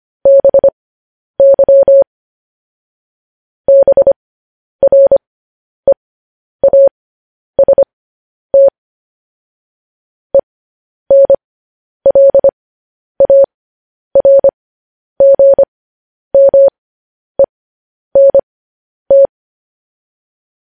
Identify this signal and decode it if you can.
Morse code, heard as a keyed tone, about 25 wpm, spaced as for 10 wpm, 550 Hz. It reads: BY BREAST ENLARGMENT
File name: podcast / News headlines-1x-25x10